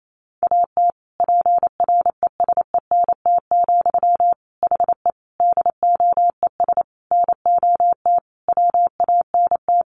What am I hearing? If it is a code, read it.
Morse code, heard as a keyed tone, about 28 wpm, 700 Hz. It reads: AT PRESENT, HE DOES NOT WANT
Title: Sentences from Top 800 Words 28wpm